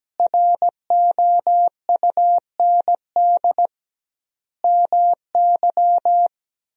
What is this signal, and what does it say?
Morse code, heard as a keyed tone, about 17 wpm, 700 Hz. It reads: ROUND MY
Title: 2x Word-Spacing - Sentences from Top 200 Words - 17wpm